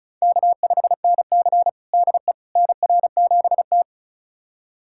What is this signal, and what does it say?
Morse code, heard as a keyed tone, about 34 wpm, 700 Hz. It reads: K5NC DE NR7T
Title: USA Callsign DE Callsign - 4 Characters - 35wpm